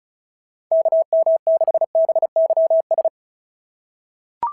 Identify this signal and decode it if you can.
Morse code, heard as a keyed tone, about 36 wpm, 660 Hz. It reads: KM6BYS
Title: SOTA QSO Practice 35wpm